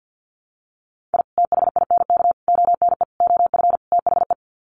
Morse code, heard as a keyed tone, about 50 wpm, 700 Hz. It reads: I THINK ONE OF THE
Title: Sets of 5 Words 50wpm